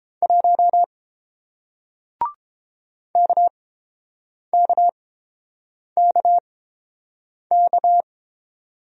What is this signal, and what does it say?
Morse code, heard as a keyed tone, about 30 wpm, spaced as for 7 wpm, 700 Hz. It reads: 1 KKKK